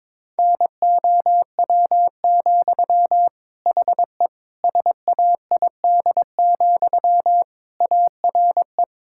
Morse code, heard as a keyed tone, about 22 wpm, 700 Hz. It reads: NOW, HE SAID, ARE
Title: Sentences from Top 1600 Words 22wpm